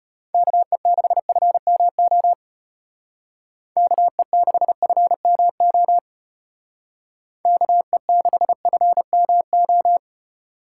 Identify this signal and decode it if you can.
Morse code, heard as a keyed tone, about 33 wpm, 700 Hz. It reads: KE6FMO KE6FMO KE6FMO